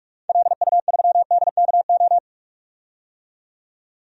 Morse code, heard as a keyed tone, about 46 wpm, 700 Hz. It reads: LU3DKO
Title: DXCC-100 Call Signs - 6 Characters - 45wpm